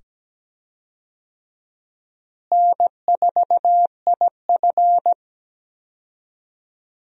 Morse code, heard as a keyed tone, about 17 wpm, 700 Hz. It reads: N4IF
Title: USA Call Signs - 4 Characters - Repeated 2x - 17wpm